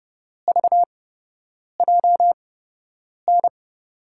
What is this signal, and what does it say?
Morse code, heard as a keyed tone, about 30 wpm, spaced as for 8 wpm, 700 Hz. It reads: VJN
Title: USA Call Signs - 6 Characters - 8x Character Spacing - 30wpm